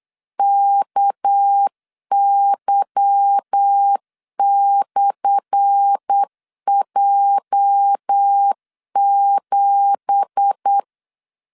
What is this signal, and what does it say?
Morse code, heard as a keyed tone, about 8 wpm, 800 Hz. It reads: KY/J7
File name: LEZIONI / 10 SIGNS / 10D COMA 8wpm 3749